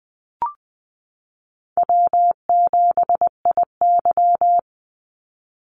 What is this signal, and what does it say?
Morse code, heard as a keyed tone, about 20 wpm, 700 Hz. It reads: W7IY
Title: Easy Call Signs Repeated 3x - 20wpm